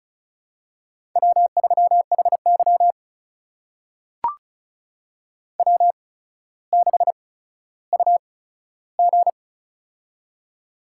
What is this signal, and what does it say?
Morse code, heard as a keyed tone, about 36 wpm, spaced as for 15 wpm, 700 Hz. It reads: W3HY W6UG
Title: USA Call Signs - 4 Characters - 8x Character Spacing - 35wpm